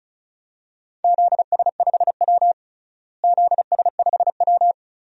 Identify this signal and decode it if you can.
Morse code, heard as a keyed tone, about 34 wpm, 700 Hz. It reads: ZS5W ZS5W
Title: DXCC-100 Call Signs - 4 Characters - Repeated 2x - 35wpm